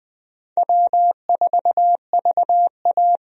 Morse code, heard as a keyed tone, about 20 wpm, 700 Hz. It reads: W4VA
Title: USA Call Signs - 4 Characters - 20wpm